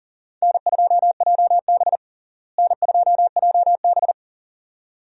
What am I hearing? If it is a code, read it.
Morse code, heard as a keyed tone, about 40 wpm, 700 Hz. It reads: N2JB N2JB
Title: USA Call Signs - 4 Characters - Repeated 2x - 40wpm